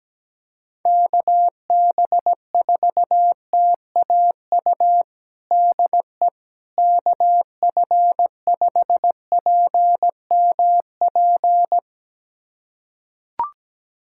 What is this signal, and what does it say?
Morse code, heard as a keyed tone, about 17 wpm, 700 Hz. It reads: KB4TAU DE KF5PMP
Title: USA Callsign DE Callsign - 6 Characters - 17wpm